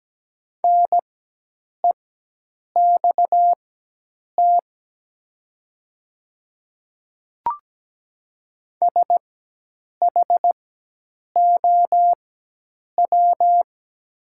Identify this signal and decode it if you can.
Morse code, heard as a keyed tone, about 17 wpm, spaced as for 8 wpm, 700 Hz. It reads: NEXT SHOW
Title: 4x Character Spacing - Top 300 Words - 17wpm